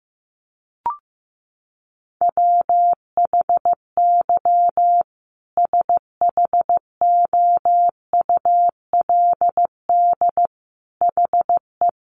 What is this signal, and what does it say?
Morse code, heard as a keyed tone, about 15 wpm, 700 Hz. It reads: WHY SHOULD HE
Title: Sentences from Top 1400 Words 15wpm